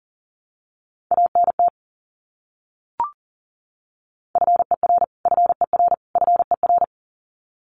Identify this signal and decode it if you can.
Morse code, heard as a keyed tone, about 40 wpm, 700 Hz. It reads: ANT FER FER FER
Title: Abbreviations Repeated 3x - 40wpm